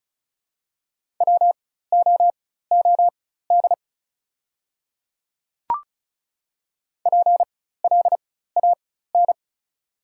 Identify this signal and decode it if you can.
Morse code, heard as a keyed tone, about 36 wpm, spaced as for 16 wpm, 700 Hz. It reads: WOOD PLAN
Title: 4x Character Spacing - Top 500 Words - 35wpm